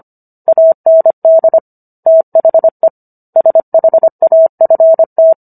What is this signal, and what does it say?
Morse code, heard as a keyed tone, about 25 wpm, 650 Hz. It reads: AND THE SHAFT